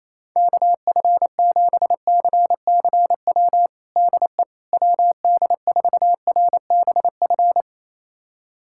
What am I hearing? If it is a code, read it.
Morse code, heard as a keyed tone, about 28 wpm, 700 Hz. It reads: KF7CCW DE WD4RBF
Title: USA Callsign DE Callsign - 6 Characters - 28wpm